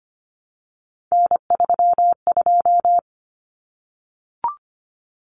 Morse code, heard as a keyed tone, about 25 wpm, 700 Hz. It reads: N32
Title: Common Cut Numbers - 3-Digit 25wpm